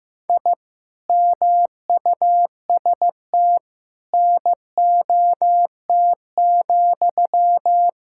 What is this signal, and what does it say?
Morse code, heard as a keyed tone, about 15 wpm, 700 Hz. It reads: I MUST NOT,
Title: Sentences from Top 1000 Words - Review Part 5 - 15wpm